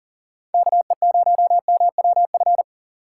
Morse code, heard as a keyed tone, about 40 wpm, 700 Hz. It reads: KE0MWF
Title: USA Call Signs - 2x Character Spacing - 40wpm